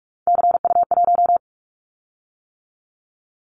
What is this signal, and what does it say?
Morse code, heard as a keyed tone, about 44 wpm, 700 Hz. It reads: CU1
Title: Callsign Prefixes for Top 100 DXCC - 45wpm